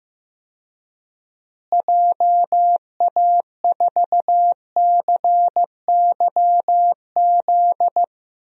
Morse code, heard as a keyed tone, about 15 wpm, 700 Hz. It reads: JA4CYZ
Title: DXCC-100 Call Signs - 6 Characters - Repeated 3x - 15wpm